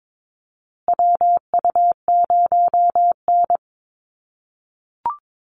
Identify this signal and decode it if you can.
Morse code, heard as a keyed tone, about 22 wpm, 700 Hz. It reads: WU0N
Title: USA Call Signs Repeated 2x 22wpm